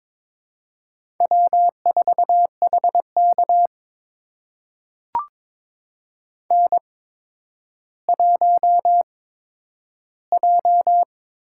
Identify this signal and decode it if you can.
Morse code, heard as a keyed tone, about 22 wpm, spaced as for 11 wpm, 700 Hz. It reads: W4HK N1J
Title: USA Call Signs - 8x Character Spacing - 22wpm